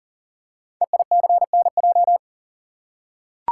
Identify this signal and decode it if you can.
Morse code, heard as a keyed tone, about 42 wpm, 700 Hz. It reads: EICNJ
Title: Code Groups - 5 Letters - 40wpm